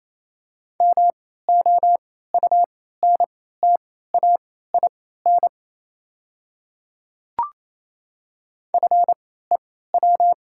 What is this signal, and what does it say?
Morse code, heard as a keyed tone, about 28 wpm, spaced as for 16 wpm, 690 Hz. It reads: MOUNTAIN FEW
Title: 3x Character Spacing - Top 300 Words - 28wpm